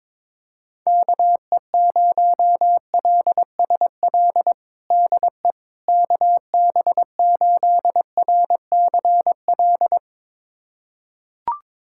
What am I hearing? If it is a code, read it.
Morse code, heard as a keyed tone, about 21 wpm, 700 Hz. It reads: KE0LSL DE KB8RCL
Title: USA Callsign DE Callsign 22wpm